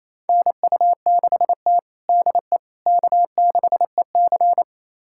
Morse code, heard as a keyed tone, about 28 wpm, 700 Hz. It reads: NU6T DE K6EC
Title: USA Callsign DE Callsign - 4 Characters - 28wpm